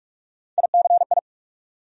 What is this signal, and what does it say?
Morse code, heard as a keyed tone, about 42 wpm, 700 Hz. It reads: ICI
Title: USA Call Sign Postfixes - 3 Letters 45wpm